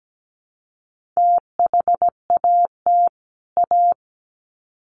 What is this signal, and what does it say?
Morse code, heard as a keyed tone, about 17 wpm, 700 Hz. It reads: THAT A
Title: Sets of 2 Words 17wpm